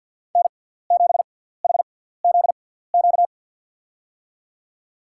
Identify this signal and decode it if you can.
Morse code, heard as a keyed tone, about 47 wpm, spaced as for 17 wpm, 700 Hz. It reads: N7HZQ
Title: USA Call Signs - 5 Characters - 6x Character Spacing - 50wpm